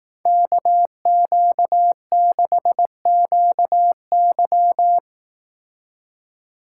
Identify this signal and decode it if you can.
Morse code, heard as a keyed tone, about 18 wpm, 700 Hz. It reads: KQ6QY
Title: Speed Racing - USA 2x2 Call Signs 15wpm